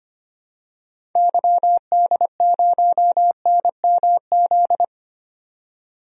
Morse code, heard as a keyed tone, about 25 wpm, 690 Hz. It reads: YD0NMZ
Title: DXCC-100 Call Signs - 6 Characters - 25wpm